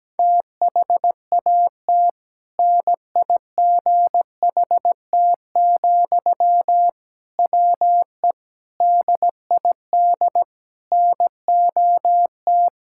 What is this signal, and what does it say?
Morse code, heard as a keyed tone, about 17 wpm, 700 Hz. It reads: THAT NIGHT, WE DID NOT
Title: Sentences from Top 1000 Words - Review Part 6 - 17wpm